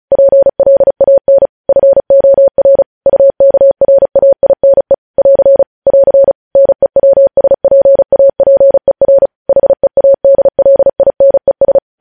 Morse code, heard as a keyed tone, about 36 wpm, 550 Hz. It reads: PLAN FOR UKRAINE <AR> <AR> NEWSPAPER HEADLINES
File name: podcast / News headlines-1x-35x35